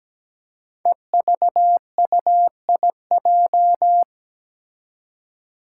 Morse code, heard as a keyed tone, about 17 wpm, 700 Hz. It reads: EVUIJ
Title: Five Challenging Letter-Number Combinations 17wpm